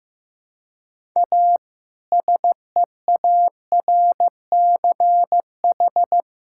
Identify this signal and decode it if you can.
Morse code, heard as a keyed tone, about 15 wpm, 700 Hz. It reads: A SEARCH
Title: Sentences from Top 1000 Words - Review Part 5 - 15wpm